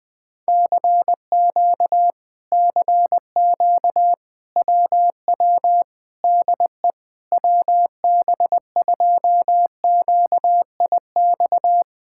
Morse code, heard as a keyed tone, about 20 wpm, 700 Hz. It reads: CQ CQ WW DE WB2QIX
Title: Speed-Racing Common QSO Phrases Encore - 17wpm